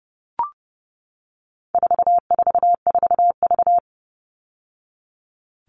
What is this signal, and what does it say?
Morse code, heard as a keyed tone, about 30 wpm, 700 Hz. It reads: 444V
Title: All Cut Numbers - 4-Digit 30wpm